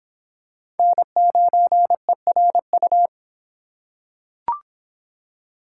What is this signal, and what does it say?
Morse code, heard as a keyed tone, about 25 wpm, 700 Hz. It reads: N9ERU